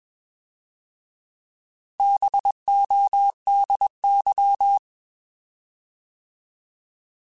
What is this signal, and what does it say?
Morse code, heard as a keyed tone, about 21 wpm, 800 Hz. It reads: BODY
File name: Alphabetical / Alphabetical mp3s / Bs @20